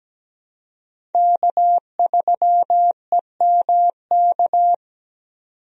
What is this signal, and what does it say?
Morse code, heard as a keyed tone, about 17 wpm, 700 Hz. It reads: K3EMK
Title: USA Call Signs - 5 Characters - 17wpm